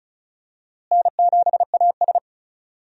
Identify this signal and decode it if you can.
Morse code, heard as a keyed tone, about 34 wpm, 700 Hz. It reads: N7AS
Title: USA Call Signs - 4 Characters - Repeated 3x - 35wpm